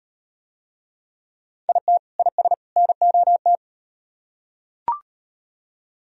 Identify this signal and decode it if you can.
Morse code, heard as a keyed tone, about 38 wpm, 700 Hz. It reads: IT IS NOT